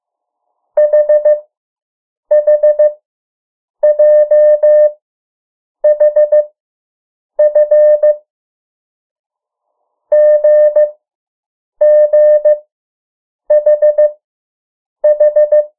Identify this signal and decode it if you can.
Morse code, heard as a keyed tone, about 15 wpm, 600 Hz. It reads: H H J H F G G H H
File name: morse mp3 / 03 urok 2